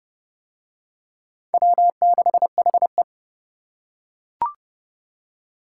Morse code, heard as a keyed tone, about 28 wpm, 700 Hz. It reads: W6HE